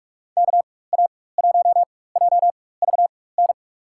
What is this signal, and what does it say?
Morse code, heard as a keyed tone, about 46 wpm, spaced as for 21 wpm, 700 Hz. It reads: KA1JVN
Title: USA Call Signs - 4x Character Spacing - 45wpm